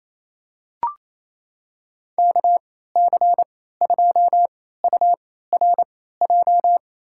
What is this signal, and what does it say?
Morse code, heard as a keyed tone, about 27 wpm, spaced as for 16 wpm, 700 Hz. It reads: KC2URJ
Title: USA Call Signs - 6 Characters - 3x Character Spacing - 28wpm